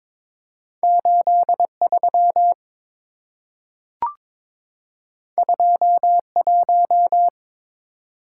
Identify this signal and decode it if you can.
Morse code, heard as a keyed tone, about 22 wpm, 700 Hz. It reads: 83 21
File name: Two Letter-Number Combinations - 22wpm